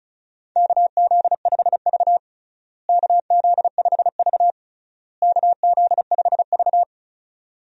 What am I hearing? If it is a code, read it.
Morse code, heard as a keyed tone, about 36 wpm, 700 Hz. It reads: KZ5V KZ5V KZ5V
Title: USA Call Signs - 4 Characters - Repeated 3x - 35wpm